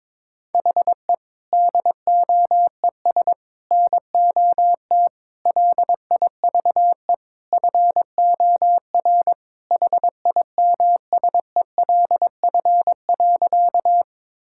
Morse code, heard as a keyed tone, about 22 wpm, 700 Hz. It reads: HE DOES NOT LIVE FOR HIMSELF.
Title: Sentences from Top 1600 Words 22wpm